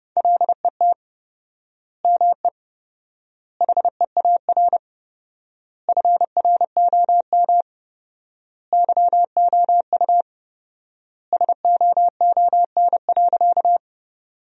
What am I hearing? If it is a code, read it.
Morse code, heard as a keyed tone, about 30 wpm, 700 Hz. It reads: LET ME HEAR FROM YOU SOON.
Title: 4x Word-Spacing - Sentences from Top 400 Words - 30wpm